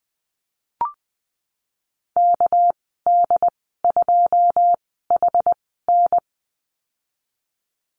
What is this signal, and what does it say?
Morse code, heard as a keyed tone, about 20 wpm, spaced as for 15 wpm, 700 Hz. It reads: KD2HN
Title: 213 Lesson 34g - Number 8 - Call Signs 20wpm